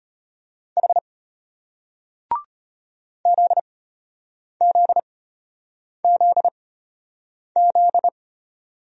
Morse code, heard as a keyed tone, about 34 wpm, spaced as for 8 wpm, 700 Hz. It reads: H ZZZZ